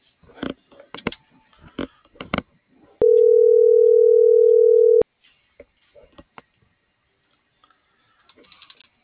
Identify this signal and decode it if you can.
Ringback tone (North America), 440 + 480 Hz.